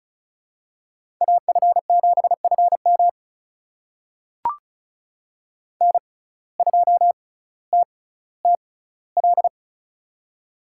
Morse code, heard as a keyed tone, about 36 wpm, spaced as for 18 wpm, 700 Hz. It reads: AF7FM N2TTL
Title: USA Call Signs - 5 Characters - 6x Character Spacing - 35wpm